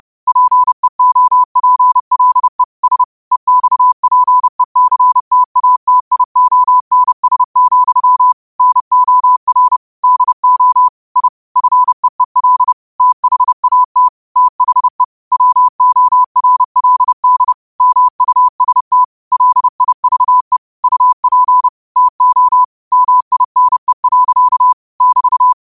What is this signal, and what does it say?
Morse code, heard as a keyed tone, about 30 wpm, 1000 Hz. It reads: PEOPLE S EXPECTATIONS, NOR DO I FEEL THAT THE WORLD MUST LIVE UP TO MINE. <BT>
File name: qotdfiles / 30wpm